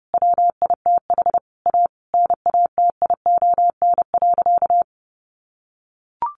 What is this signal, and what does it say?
Morse code, heard as a keyed tone, about 28 wpm, 700 Hz. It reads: WITH A NATION.
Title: Sentences from Top 1200 Words 30wpm